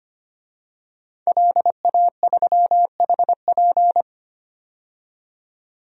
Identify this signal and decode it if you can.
Morse code, heard as a keyed tone, about 25 wpm, 700 Hz. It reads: LA3HP
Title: DXCC-100 Call Signs - 5 Characters - 25wpm